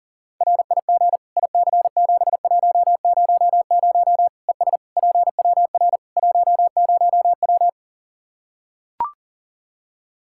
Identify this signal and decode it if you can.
Morse code, heard as a keyed tone, about 40 wpm, 700 Hz. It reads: RIG IC7100 ES PWR 10W
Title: K1USN SST QSO Practice 40wpm